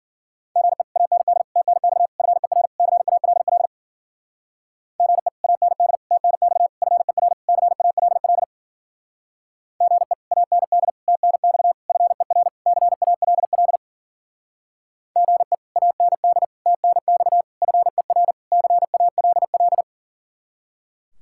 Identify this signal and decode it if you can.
Morse code, heard as a keyed tone, about 50 wpm, 700 Hz. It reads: GE AND TNX FER CALL GE AND TNX FER CALL GE AND TNX FER CALL GE AND TNX FER CALL